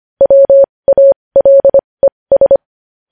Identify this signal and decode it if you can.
Morse code, heard as a keyed tone, about 25 wpm, spaced as for 20 wpm, 550 Hz. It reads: WALES
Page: news-headlines-in-morse-code-at-20-wpm